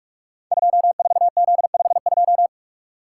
Morse code, heard as a keyed tone, about 44 wpm, 700 Hz. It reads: 24Z52
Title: Code Groups - 5 Letter:Numbers - 45wpm